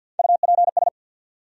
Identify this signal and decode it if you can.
Morse code, heard as a keyed tone, about 46 wpm, 700 Hz. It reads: UPS
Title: USA Call Sign Postfixes - 3 Letters 50wpm